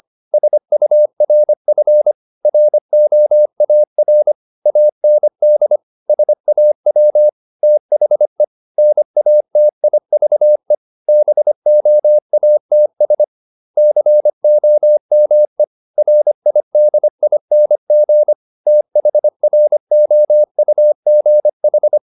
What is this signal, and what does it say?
Morse code, heard as a keyed tone, about 25 wpm, 600 Hz. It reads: SURF ROAR AND SAW THE NATIVE BOATS COME RIDING THROUGH
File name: the old man and the sea part 1 25wpm